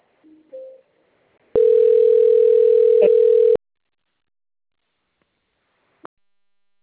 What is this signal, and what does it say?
Ringback tone (North America), 440 + 480 Hz.